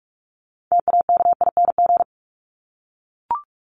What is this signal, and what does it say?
Morse code, heard as a keyed tone, about 44 wpm, 700 Hz. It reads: TAKING
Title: ING Suffix - Top 100 Words with Base Words 45wpm